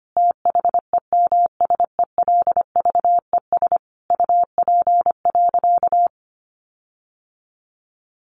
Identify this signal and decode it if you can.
Morse code, heard as a keyed tone, about 25 wpm, 700 Hz. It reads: THEMSELVES UP.
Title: Sentences from Top 1400 Words 25wpm